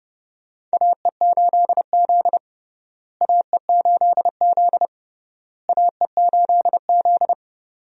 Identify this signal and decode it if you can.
Morse code, heard as a keyed tone, about 32 wpm, 700 Hz. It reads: AE8Z AE8Z AE8Z
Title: USA Call Signs - 4 Characters - Repeated 3x - 30wpm